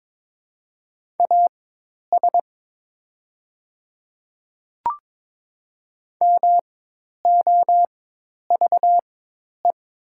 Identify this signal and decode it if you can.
Morse code, heard as a keyed tone, about 22 wpm, spaced as for 10 wpm, 700 Hz. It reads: AS MOVE
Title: 4x Character Spacing - Top 200 Words - 22wpm